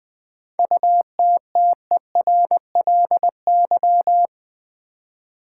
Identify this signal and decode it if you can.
Morse code, heard as a keyed tone, about 20 wpm, 700 Hz. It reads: UTTERLY
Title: 7 Letter Words - First 3 Letters Given - 20wpm